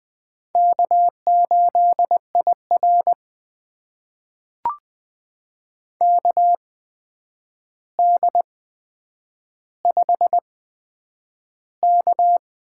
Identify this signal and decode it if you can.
Morse code, heard as a keyed tone, about 20 wpm, spaced as for 9 wpm, 700 Hz. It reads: K8IR KD5K
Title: USA Call Signs - 4 Characters - 8x Character Spacing - 20wpm